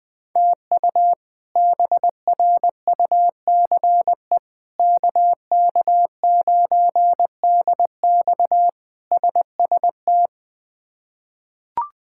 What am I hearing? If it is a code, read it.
Morse code, heard as a keyed tone, about 20 wpm, 700 Hz. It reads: TU BRUCE KK9DX SST
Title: K1USN SST QSO Practice - 20wpm